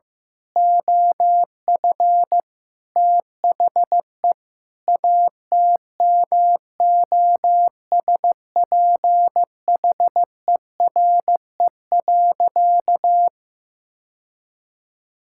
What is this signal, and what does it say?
Morse code, heard as a keyed tone, about 15 wpm, 700 Hz. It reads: OF THE ATMOSPHERE.
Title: Sentences from Top 1600 Words 15wpm